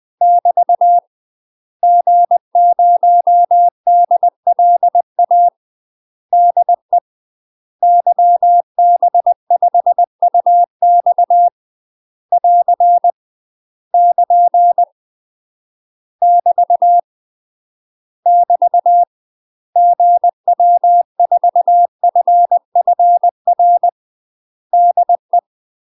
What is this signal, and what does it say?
Morse code, heard as a keyed tone, about 20 wpm, 700 Hz. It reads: <BT> G0DLA DE YB5UX <AR> <KN> <BT> <BT> GW4FFR DE
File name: UK MASTER QSOs TRAINING / 20WPM / UK MASTER QSOs 0034